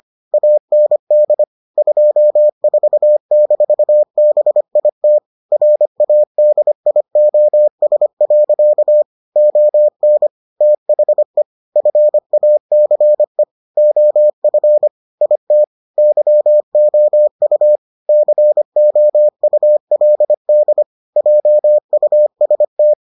Morse code, heard as a keyed tone, about 25 wpm, 600 Hz. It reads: AND 24-BIT RADIOS. ON THE FACE OF IT YOU COULD JUST